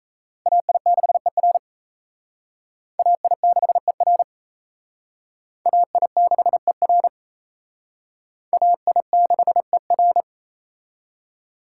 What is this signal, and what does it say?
Morse code, heard as a keyed tone, about 36 wpm, 700 Hz. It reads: AI6ER AI6ER AI6ER AI6ER